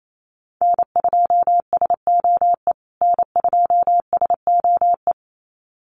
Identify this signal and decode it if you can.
Morse code, heard as a keyed tone, about 28 wpm, 700 Hz. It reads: N2SOE N2SOE
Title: USA Call Signs Repeated 2x 28wpm